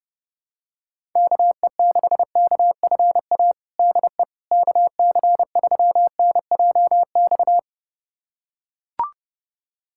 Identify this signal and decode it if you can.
Morse code, heard as a keyed tone, about 28 wpm, 700 Hz. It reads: KE6KFA DE KC3NJX
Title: USA Callsign DE Callsign - 6 Characters - 30wpm